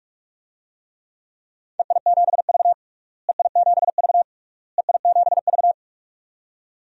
Morse code, heard as a keyed tone, about 48 wpm, 700 Hz. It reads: EI7V EI7V EI7V
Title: DXCC-100 Call Signs - 4 Characters - Repeated 3x - 45wpm